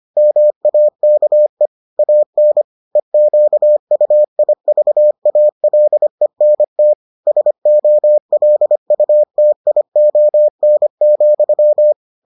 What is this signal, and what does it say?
Morse code, heard as a keyed tone, about 25 wpm, 600 Hz. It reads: MAKE AN EQUIVALENT SOLUTION,